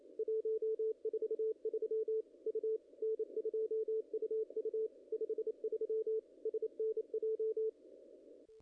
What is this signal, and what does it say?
Morse code, heard as a keyed tone, about 30 wpm, 440 Hz. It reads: 143 U N2UU 53 SNJ
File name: contests / ss-cw-k5tr-op-n5rz -0000